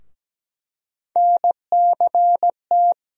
Morse code, heard as a keyed tone, about 17 wpm, 700 Hz. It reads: NCT
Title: USA Call Sign Postfixes - 3 Letters 17wpm